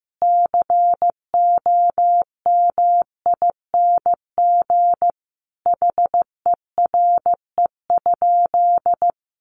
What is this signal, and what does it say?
Morse code, heard as a keyed tone, about 15 wpm, 700 Hz. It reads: COMING HERE?
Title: Sentences from Top 1300 Words 15wpm